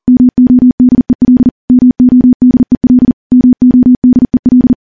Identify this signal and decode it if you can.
Morse code, heard as a keyed tone, about 40 wpm, 250 Hz. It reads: MODEL MODEL MODEL
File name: morse / morse-training-40WPM